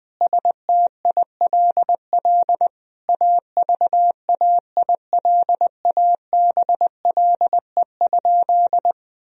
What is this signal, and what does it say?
Morse code, heard as a keyed tone, about 20 wpm, 700 Hz. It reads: STILL AVAILABLE?
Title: Sentences from Top 1000 Words - Review Part 3 - 20wpm